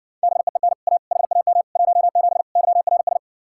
Morse code, heard as a keyed tone, about 60 wpm, 700 Hz. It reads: BEEN A HAM 17 YRS